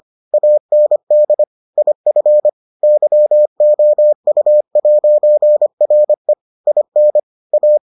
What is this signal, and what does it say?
Morse code, heard as a keyed tone, about 25 wpm, 600 Hz. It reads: AND IF YOU'RE IN A